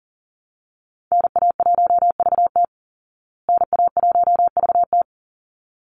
Morse code, heard as a keyed tone, about 38 wpm, 700 Hz. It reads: NA1VT NA1VT
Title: Call Signs Repeated 3x - 40wpm